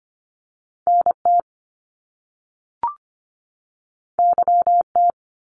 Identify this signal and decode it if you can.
Morse code, heard as a keyed tone, about 25 wpm, 700 Hz. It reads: NT YT
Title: Canadian Provinces and Territories 25wpm